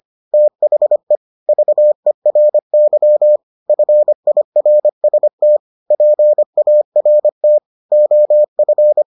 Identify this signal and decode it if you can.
Morse code, heard as a keyed tone, about 25 wpm, 600 Hz. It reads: THE VERY FIRST PART OF